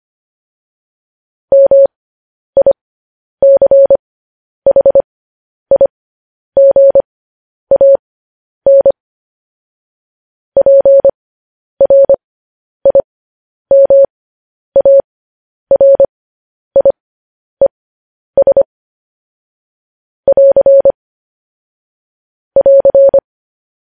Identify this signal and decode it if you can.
Morse code, heard as a keyed tone, about 25 wpm, spaced as for 10 wpm, 550 Hz. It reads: MICHIGAN PRIMARIES <AR> <AR>